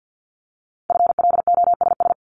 Morse code, heard as a keyed tone, about 50 wpm, 700 Hz. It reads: FLOSS
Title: Multi-letter Phonemes Repeated 3x 50wpm